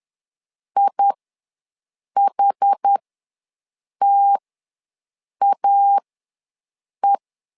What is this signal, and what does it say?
Morse code, heard as a keyed tone, about 10 wpm, spaced as for 6 wpm, 800 Hz. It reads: IHTAE
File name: LEZIONI / 2 UHDN / 2 UHD ws 10wpm 2058